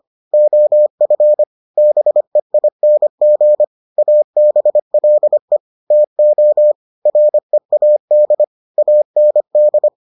Morse code, heard as a keyed tone, about 25 wpm, 600 Hz. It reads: OF BEING ABLE TO READ AND